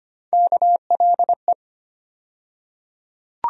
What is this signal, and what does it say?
Morse code, heard as a keyed tone, about 25 wpm, 700 Hz. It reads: KLE
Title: USA Call Sign Postfixes - 3 Letters 25wpm